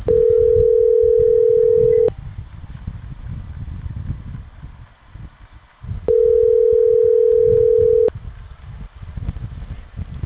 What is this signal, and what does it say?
Ringback tone (North America), 440 + 480 Hz.